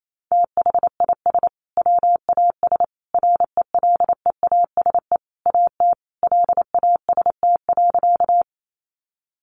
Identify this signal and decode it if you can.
Morse code, heard as a keyed tone, about 28 wpm, 700 Hz. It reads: THIS WAS RELEASE AT LAST.
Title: Sentences from Top 1100 Words 28wpm